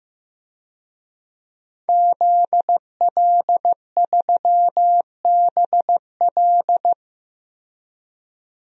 Morse code, heard as a keyed tone, about 15 wpm, 700 Hz. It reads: ZL3BL
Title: DXCC-100 Call Signs - 5 Characters - 15wpm